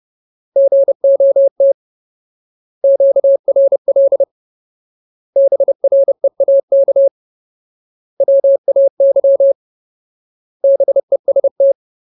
Morse code, heard as a keyed tone, about 30 wpm, 550 Hz. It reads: GOT QRL BREAK WAY BEST
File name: Ultimate Word List / 30WPM Ultimate